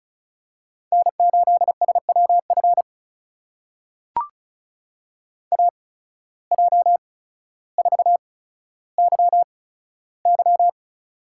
Morse code, heard as a keyed tone, about 36 wpm, spaced as for 15 wpm, 700 Hz. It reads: N8SWF AJ4YY
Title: USA Call Signs - 5 Characters - 8x Character Spacing - 35wpm